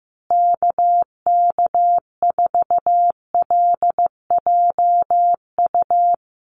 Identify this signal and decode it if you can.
Morse code, heard as a keyed tone, about 15 wpm, 700 Hz. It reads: KK4LJU
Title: USA Call Signs Repeated 2x 15wpm